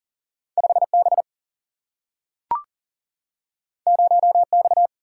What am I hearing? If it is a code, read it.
Morse code, heard as a keyed tone, about 40 wpm, 700 Hz. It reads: 5 B 0 X
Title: Single-Letter Single-Number 40wpm